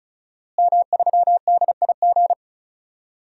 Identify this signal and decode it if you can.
Morse code, heard as a keyed tone, about 36 wpm, 700 Hz. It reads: M3DIG
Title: DXCC-100 Call Signs - 5 Characters - Repeated 3x - 35wpm